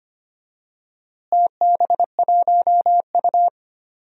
Morse code, heard as a keyed tone, about 25 wpm, 700 Hz. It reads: TB1U
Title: DXCC-100 Call Signs - 4 Characters - Repeated 2x - 25wpm